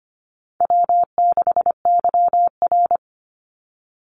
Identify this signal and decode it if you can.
Morse code, heard as a keyed tone, about 25 wpm, 700 Hz. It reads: W6YR
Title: Easy Call Signs Repeated 2x - 25wpm